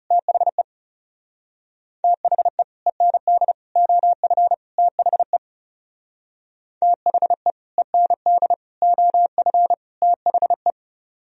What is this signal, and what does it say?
Morse code, heard as a keyed tone, about 40 wpm, 700 Hz. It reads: THE THE END OF THE THE END OF THE